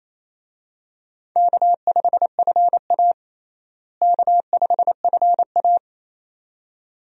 Morse code, heard as a keyed tone, about 28 wpm, 700 Hz. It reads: K5FA K5FA
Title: USA Call Signs - 4 Characters - Repeated 2x - 28wpm